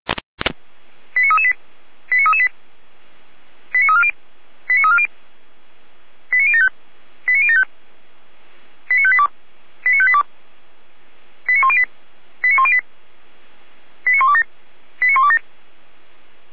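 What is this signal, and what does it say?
Five-tone selective call (ZVEI), heard as quick runs of five tones, tones of about 70 ms. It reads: 89338 89338 89340 89340 89075 89075 89582 89582 89108 89108 89127 89127
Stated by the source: Five-tone ZVEI selective call received by SDR, recorded by feuerwehr-orlamuende fire brigade.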